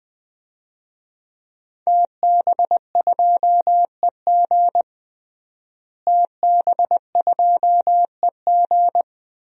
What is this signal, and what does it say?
Morse code, heard as a keyed tone, about 20 wpm, 700 Hz. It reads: TB2EG TB2EG
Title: DXCC-100 Call Signs - 5 Characters - Repeated 3x - 20wpm